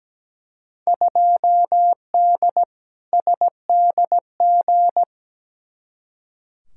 Morse code, heard as a keyed tone, about 17 wpm, 700 Hz. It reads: 2D SDG
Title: Field Day Exchanges Encore - 17wpm